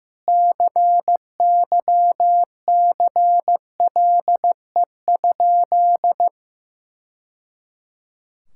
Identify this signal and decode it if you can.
Morse code, heard as a keyed tone, about 15 wpm, 700 Hz. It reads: CYCLE?
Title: Sentences from Top 1000 Words - Review Part 6 - 15wpm